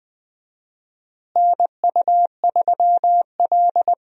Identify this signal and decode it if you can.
Morse code, heard as a keyed tone, about 20 wpm, 700 Hz. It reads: NU3L
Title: USA Call Signs - 4 Characters - Repeated 3x - 20wpm